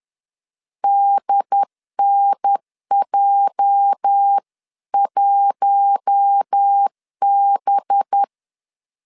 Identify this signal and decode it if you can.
Morse code, heard as a keyed tone, about 10 wpm, 800 Hz. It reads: DNJ1B